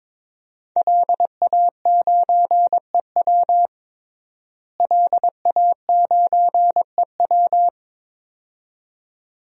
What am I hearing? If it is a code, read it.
Morse code, heard as a keyed tone, about 22 wpm, 700 Hz. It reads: LA9EW LA9EW
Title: DXCC-100 Call Signs - 5 Characters - Repeated 2x - 22wpm